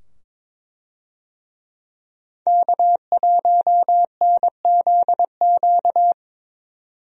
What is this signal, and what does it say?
Morse code, heard as a keyed tone, about 22 wpm, 700 Hz. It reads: K1NZQ
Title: USA Call Signs - 5 Characters - 22wpm